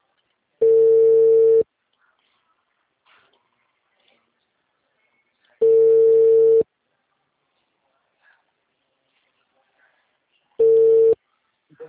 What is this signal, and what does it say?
Ringback tone (China, 450 Hz).